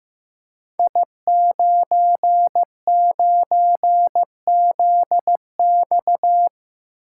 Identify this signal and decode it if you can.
Morse code, heard as a keyed tone, about 15 wpm, 700 Hz. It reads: I99ZX
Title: Code Groups - 5 Letter:Numbers - 15wpm